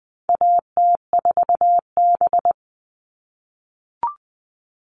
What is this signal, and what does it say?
Morse code, heard as a keyed tone, about 20 wpm, 700 Hz. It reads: AT4B
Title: All Cut Numbers - 4-Digit 20wpm